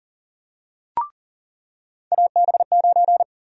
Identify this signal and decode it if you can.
Morse code, heard as a keyed tone, about 40 wpm, 700 Hz. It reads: AB9
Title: USA Call Sign Prefixes - 40wpm